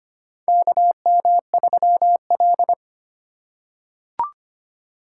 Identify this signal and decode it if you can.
Morse code, heard as a keyed tone, about 25 wpm, 700 Hz. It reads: KM3L
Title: USA Call Signs - 4 Characters - 4x Character Spacing - 25wpm